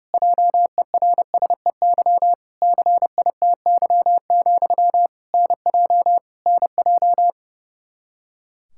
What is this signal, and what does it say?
Morse code, heard as a keyed tone, about 30 wpm, 700 Hz. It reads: JERSEY CITY, NJ NJ
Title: Practice QSO - Exchange 1 - Beginning - 30wpm